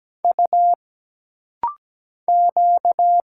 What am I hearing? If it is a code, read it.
Morse code, heard as a keyed tone, about 17 wpm, 700 Hz. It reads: U Q
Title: Single Letters - Rapid-Fire 17wpm